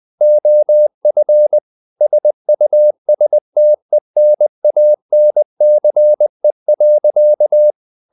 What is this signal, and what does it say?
Morse code, heard as a keyed tone, about 20 wpm, 600 Hz. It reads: OF SUSTENANCE.
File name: the old man and the sea part 2 20wpm